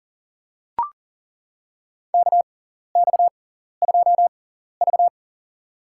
Morse code, heard as a keyed tone, about 40 wpm, spaced as for 14 wpm, 700 Hz. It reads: KX2V
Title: USA Call Signs - 4 Characters - 6x Character Spacing - 40wpm